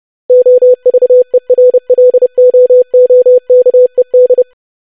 Morse code, heard as a keyed tone, about 30 wpm, 500 Hz. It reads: OVERLOOKED
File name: longwords-25wpm